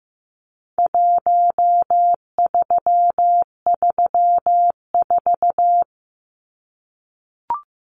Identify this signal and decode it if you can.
Morse code, heard as a keyed tone, about 15 wpm, 700 Hz. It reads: 1334
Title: Numbers - Contest Style - 4-Digit 15wpm